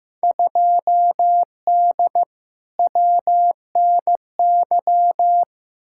Morse code, heard as a keyed tone, about 15 wpm, 700 Hz. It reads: 2D WNY
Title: Field Day Exchanges - 15wpm